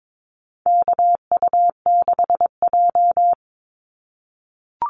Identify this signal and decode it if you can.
Morse code, heard as a keyed tone, about 22 wpm, 700 Hz. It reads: KU6J
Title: NA SOTA Activator Callsigns 22wpm